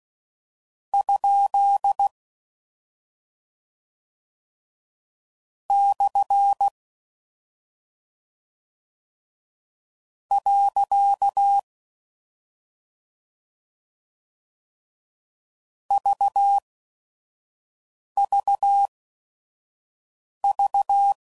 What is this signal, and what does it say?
Morse code, heard as a keyed tone, about 16 wpm, spaced as for 5 wpm, 800 Hz. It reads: ? / . VVV
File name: Mixed / Misc / Punct